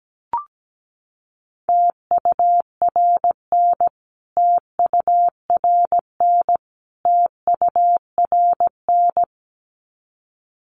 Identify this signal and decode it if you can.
Morse code, heard as a keyed tone, about 17 wpm, 700 Hz. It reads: TURN TURN TURN
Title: Top 200 Words Repeated 17wpm